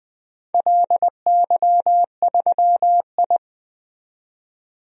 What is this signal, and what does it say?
Morse code, heard as a keyed tone, about 20 wpm, 700 Hz. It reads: LY3I
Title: DXCC-100 Call Signs - 4 Characters - 20wpm